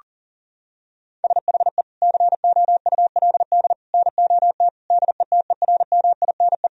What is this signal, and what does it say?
Morse code, heard as a keyed tone, about 40 wpm, 700 Hz. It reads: SHE COULD NOT DETERMINE
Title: Sentences from Top 1000 Words - Review Part 4 - 40wpm